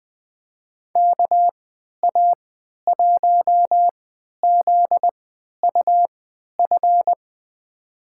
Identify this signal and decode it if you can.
Morse code, heard as a keyed tone, about 20 wpm, spaced as for 11 wpm, 700 Hz. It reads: KA1ZUF
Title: USA Call Signs - 6 Characters - 3x Character Spacing - 20wpm